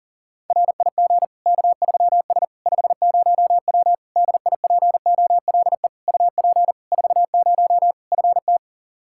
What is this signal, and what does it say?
Morse code, heard as a keyed tone, about 39 wpm, 700 Hz. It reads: RIG K3S 50W DIPOLE UP 40 FT
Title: K1USN SST QSO Practice 40wpm